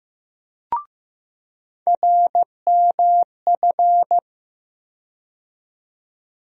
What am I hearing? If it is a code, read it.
Morse code, heard as a keyed tone, about 15 wpm, 700 Hz. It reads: RMF
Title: USA Call Sign Postfixes - 3 Letters 15wpm